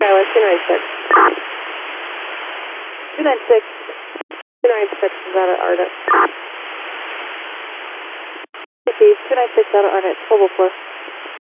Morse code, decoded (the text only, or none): none